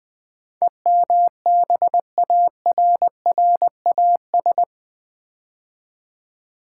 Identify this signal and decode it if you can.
Morse code, heard as a keyed tone, about 20 wpm, 700 Hz. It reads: EMBARRAS
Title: Build A Word - 4 Letters At A Time - 11-Letter Words - 20wpm